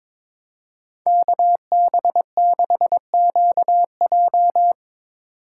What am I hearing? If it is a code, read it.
Morse code, heard as a keyed tone, about 24 wpm, 700 Hz. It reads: KB6QJ
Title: Speed Racing - USA 2x2 Call Signs 22wpm